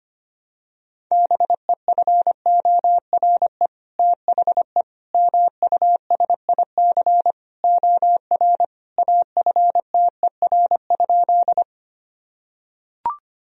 Morse code, heard as a keyed tone, about 25 wpm, 700 Hz. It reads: BEFORE THE MUSIC OR AFTER?
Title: Sentences from Top 1000 Words - Review Part 1 - 25wpm